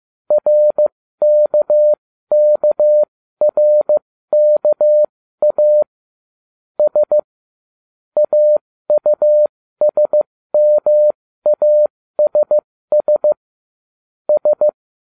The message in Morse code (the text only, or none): RKKRKA S AUSMASS S